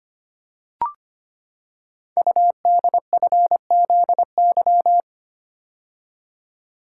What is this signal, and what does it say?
Morse code, heard as a keyed tone, about 25 wpm, 700 Hz. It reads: UDFZY
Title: Code Groups - 5 Letters - 25wpm